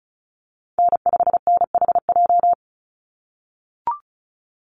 Morse code, heard as a keyed tone, about 34 wpm, 700 Hz. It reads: N5NHJ
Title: Call Signs - 35wpm